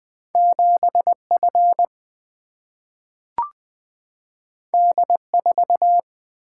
Morse code, heard as a keyed tone, about 20 wpm, 700 Hz. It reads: 7 F D 4
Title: Single-Letter Single-Number 20wpm